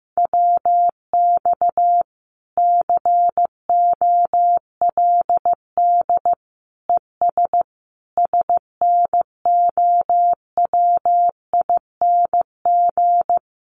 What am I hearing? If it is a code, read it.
Morse code, heard as a keyed tone, about 14 wpm, 700 Hz. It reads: WX COLD ES SNOWING
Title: Common QSO Phrases 15wpm